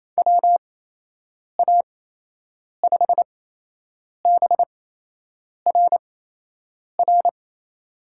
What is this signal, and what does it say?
Morse code, heard as a keyed tone, about 28 wpm, spaced as for 8 wpm, 700 Hz. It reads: WA5BRR
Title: USA Call Signs - 8x Character Spacing - 28wpm